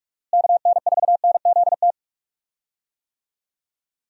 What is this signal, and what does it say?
Morse code, heard as a keyed tone, about 47 wpm, 700 Hz. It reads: KN4NZT
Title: USA Call Signs - 6 Characters - Repeated 3x - 45wpm